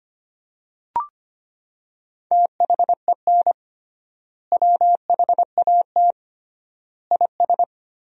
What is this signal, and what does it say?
Morse code, heard as a keyed tone, about 25 wpm, 700 Hz. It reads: THEN WHAT IS
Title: 3x Word-Spacing - Sentences from Top 100 Words - 25wpm